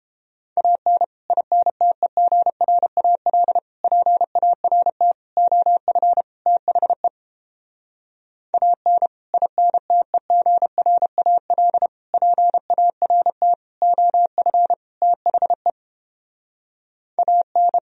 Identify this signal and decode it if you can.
Morse code, heard as a keyed tone, about 29 wpm, 700 Hz. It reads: AN INTEGRAL PART OF THE AN INTEGRAL PART OF THE AN